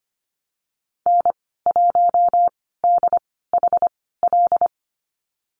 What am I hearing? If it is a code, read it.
Morse code, heard as a keyed tone, about 25 wpm, spaced as for 16 wpm, 700 Hz. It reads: N1DHL
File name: Lesson 15g - Number 5 - Call Signs 25wpm